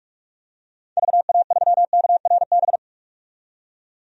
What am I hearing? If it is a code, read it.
Morse code, heard as a keyed tone, about 43 wpm, 700 Hz. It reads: VA3KRB
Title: DXCC-100 Call Signs - 6 Characters - 45wpm